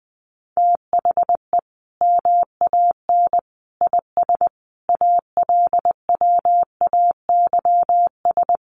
Morse code, heard as a keyed tone, about 19 wpm, 700 Hz. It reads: THE MAN IS ALWAYS
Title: Sentences from Top 1400 Words 20wpm